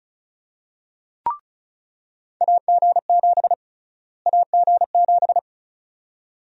Morse code, heard as a keyed tone, about 34 wpm, 700 Hz. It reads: AG7 AG7
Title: USA Call Sign Prefixes - Repeated 2x - 35wpm